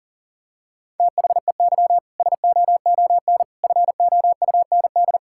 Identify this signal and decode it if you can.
Morse code, heard as a keyed tone, about 40 wpm, 700 Hz. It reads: THEY SOON FOUND
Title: Sentences from Top 1000 Words - Review Part 5 - 40wpm